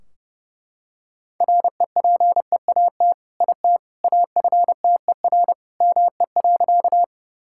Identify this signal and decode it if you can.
Morse code, heard as a keyed tone, about 30 wpm, 700 Hz. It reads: REPEAT IT AFTER ME.
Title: Sentences from Top 1600 Words 30wpm